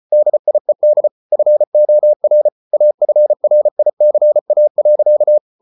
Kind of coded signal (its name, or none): Morse code